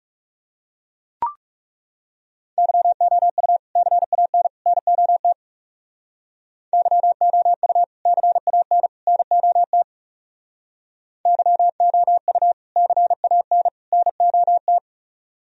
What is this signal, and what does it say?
Morse code, heard as a keyed tone, about 38 wpm, 700 Hz. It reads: YOU CAN NOT YOU CAN NOT YOU CAN NOT